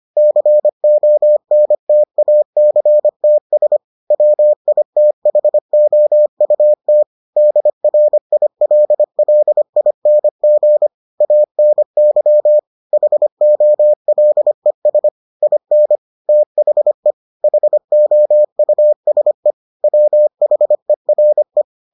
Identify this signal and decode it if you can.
Morse code, heard as a keyed tone, about 25 wpm, 600 Hz. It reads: CONTACTS WITHOUT DRILLING ANY HOLES IN THE HOUSE WHERE